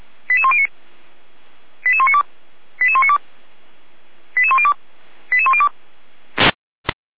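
Five-tone selective call (ZVEI), heard as quick runs of five tones, tones of about 70 ms. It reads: 88109 88192 88192 88102 88102
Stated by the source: ZVEI five-tone selective call received by SDR, from feuerwehr-orlamuende.org fire brigade server.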